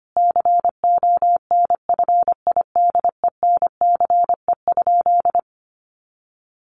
Morse code, heard as a keyed tone, about 25 wpm, 700 Hz. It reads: CONFIDENCE?
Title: Sentences from Top 1500 Words 25wpm